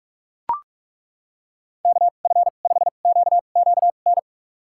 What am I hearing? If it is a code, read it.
Morse code, heard as a keyed tone, about 42 wpm, spaced as for 31 wpm, 700 Hz. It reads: KF5QQN
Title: USA Call Signs - 6 Characters - 2x Character Spacing - 45wpm